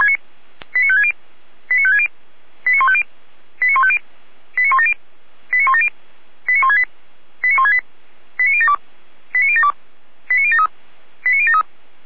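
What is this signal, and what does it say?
Five-tone selective call (ZVEI), heard as quick runs of five tones, tones of about 70 ms. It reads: 89560 89560 89150 89150 89170 89170 89168 89168 89072 89072 89073 89073
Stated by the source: ZVEI code 89167 received on SDR, feuerwehr-orlamuende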